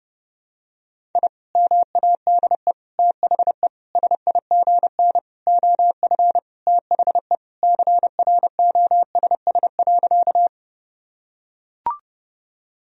Morse code, heard as a keyed tone, about 32 wpm, 700 Hz. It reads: I MADE THE SIGN OF THE CROSS.
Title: Sentences from Top 1000 Words - Review Part 1 - 30wpm